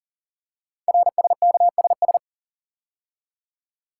Morse code, heard as a keyed tone, about 42 wpm, 690 Hz. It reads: RSKSS
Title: Code Groups - 5 Letters - 40wpm